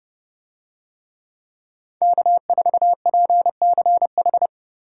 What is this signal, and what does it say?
Morse code, heard as a keyed tone, about 30 wpm, 700 Hz. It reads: K4PCH